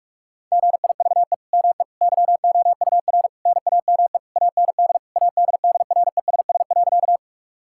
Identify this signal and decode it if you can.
Morse code, heard as a keyed tone, about 42 wpm, 700 Hz. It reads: GIVE ME YOUR NAME AND ADDRESS.
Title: Sentences from Top 1000 Words - Review Part 6 - 45wpm